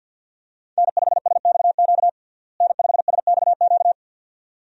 Morse code, heard as a keyed tone, about 48 wpm, 700 Hz. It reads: N5SXQ N5SXQ
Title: USA Call Signs - 5 Characters - Repeated 2x - 50wpm